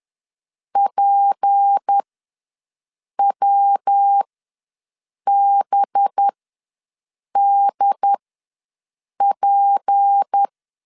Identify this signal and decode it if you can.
Morse code, heard as a keyed tone, about 10 wpm, spaced as for 6 wpm, 800 Hz. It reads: PWBDP
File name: LEZIONI / 6 PKX0 / 6 P ws 10wpm 3751